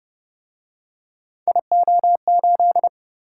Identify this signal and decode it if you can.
Morse code, heard as a keyed tone, about 31 wpm, 700 Hz. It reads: IO8